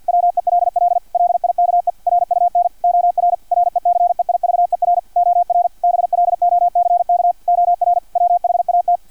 Morse code, heard as a keyed tone, about 50 wpm, 700 Hz. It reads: WELL LIKE RUN OR RECEIVER OR BLOCK OR WHAT
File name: archived / CRSnet